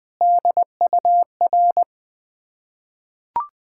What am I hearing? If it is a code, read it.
Morse code, heard as a keyed tone, about 20 wpm, 700 Hz. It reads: DUR
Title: Ontario QSO Party Exchange 20wpm